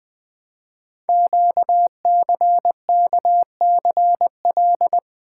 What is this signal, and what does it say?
Morse code, heard as a keyed tone, about 20 wpm, 700 Hz. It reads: QCKCL
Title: Code Groups - 5 Letters - 20wpm